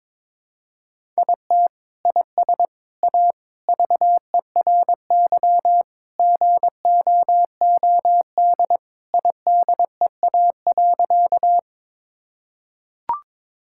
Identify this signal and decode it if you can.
Morse code, heard as a keyed tone, about 22 wpm, 700 Hz. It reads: IT IS A VERY GOOD IDEA.
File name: practice / 22wpm / Sentences with 3-Word Phrases - 22wpm